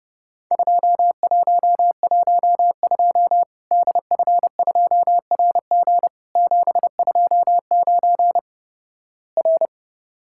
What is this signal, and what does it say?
Morse code, heard as a keyed tone, about 29 wpm, 700 Hz. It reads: 2112 DF2RG 729 R
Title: Worked All Europe DX Contest - QTC Exchange - 30wpm